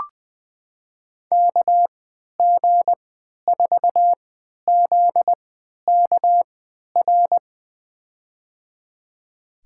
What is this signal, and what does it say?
Morse code, heard as a keyed tone, about 20 wpm, spaced as for 11 wpm, 700 Hz. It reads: KG4ZKR
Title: USA Call Signs - 6 Characters - 3x Character Spacing - 20wpm